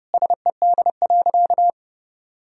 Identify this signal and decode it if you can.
Morse code, heard as a keyed tone, about 30 wpm, 700 Hz. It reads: SED.
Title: Sentences from Top 100 Words and Abbreviations 30wpm